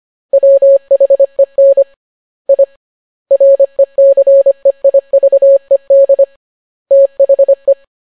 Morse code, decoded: WHEN I RECEIVED THE